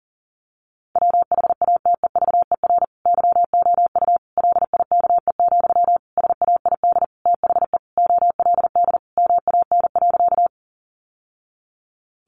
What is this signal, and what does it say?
Morse code, heard as a keyed tone, about 40 wpm, 700 Hz. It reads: WHATEVER YOU LIKE, SAID THE OLD MAN.
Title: Sentences from Top 1100 Words 40wpm